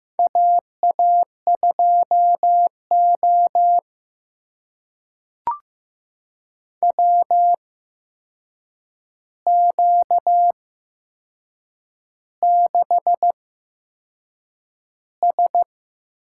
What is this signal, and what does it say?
Morse code, heard as a keyed tone, about 15 wpm, spaced as for 6 wpm, 700 Hz. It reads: AA2O WQ6S
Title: USA Call Signs - 4 Characters - 8x Character Spacing - 15wpm